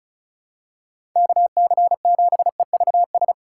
Morse code, heard as a keyed tone, about 34 wpm, 700 Hz. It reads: KC7EVS
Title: USA Call Signs - 6 Characters - 4x Character Spacing - 35wpm